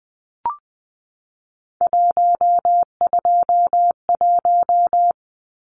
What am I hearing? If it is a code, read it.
Morse code, heard as a keyed tone, about 20 wpm, 700 Hz. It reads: 121
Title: Numbers - 3-Digit 20wpm